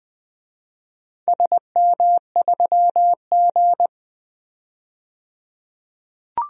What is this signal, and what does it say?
Morse code, heard as a keyed tone, about 20 wpm, 700 Hz. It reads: SM3G
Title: DXCC-100 Call Signs - 4 Characters - 20wpm